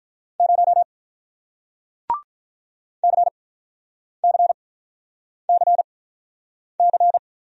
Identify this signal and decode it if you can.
Morse code, heard as a keyed tone, about 47 wpm, spaced as for 9 wpm, 700 Hz. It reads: 0 CCCC